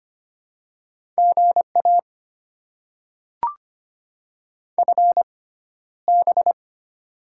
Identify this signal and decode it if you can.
Morse code, heard as a keyed tone, about 25 wpm, spaced as for 13 wpm, 700 Hz. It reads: GA FB
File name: Top 20 QSO Elements - 6x Character Spacing - 25wpm